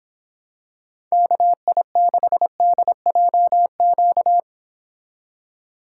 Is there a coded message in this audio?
Yes: Morse code from 1.1 to 4.4 s, heard as a keyed tone, about 26 wpm, 700 Hz. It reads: KI6DJQ